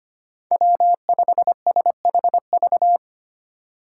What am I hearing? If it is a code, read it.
Morse code, heard as a keyed tone, about 25 wpm, 700 Hz. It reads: W5SHV
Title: USA Call Signs - 5 Characters - 3x Character Spacing - 25wpm